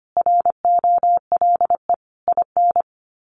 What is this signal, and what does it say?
Morse code, heard as a keyed tone, about 25 wpm, 700 Hz. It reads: ROLE IN
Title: Sets of 5 Words Encore2 25wpm